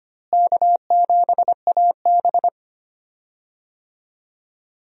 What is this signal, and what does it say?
Morse code, heard as a keyed tone, about 25 wpm, 700 Hz. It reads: K7AB
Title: USA Call Signs - 4 Characters - Repeated 2x - 25wpm